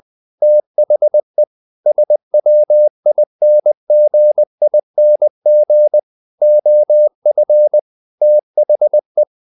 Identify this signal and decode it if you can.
Morse code, heard as a keyed tone, about 20 wpm, 600 Hz. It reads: THE SWINGING OF THE
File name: the old man and the sea part 2 20wpm